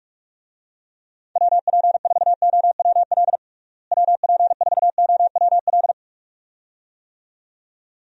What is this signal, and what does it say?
Morse code, heard as a keyed tone, about 46 wpm, 700 Hz. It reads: WP4OWL WP4OWL
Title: USA Call Signs - 6 Characters - Repeated 2x - 45wpm